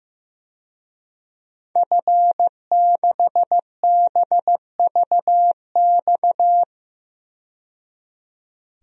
Morse code, heard as a keyed tone, about 15 wpm, 700 Hz. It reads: F6BVX
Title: DXCC-100 Call Signs - 5 Characters - 15wpm